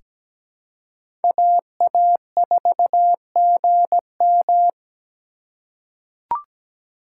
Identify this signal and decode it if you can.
Morse code, heard as a keyed tone, about 17 wpm, 700 Hz. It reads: AA4GM
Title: USA Call Signs - 5 Characters - 4x Character Spacing - 17wpm